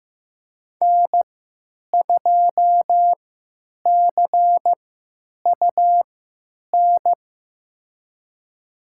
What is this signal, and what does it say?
Morse code, heard as a keyed tone, about 15 wpm, spaced as for 8 wpm, 700 Hz. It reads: N2CUN
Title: USA Call Signs - 5 Characters - 3x Character Spacing - 15wpm